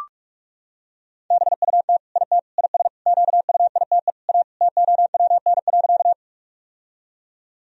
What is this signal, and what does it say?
Morse code, heard as a keyed tone, about 41 wpm, 700 Hz. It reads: BUT IT IS QUITE A TOWN.
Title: Sentences from Top 1000 Words - Review Part 2 - 45wpm